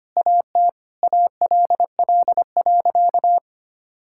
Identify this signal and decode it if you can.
Morse code, heard as a keyed tone, about 25 wpm, 700 Hz. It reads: AT ALL.
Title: Sentences from Top 1600 Words 25wpm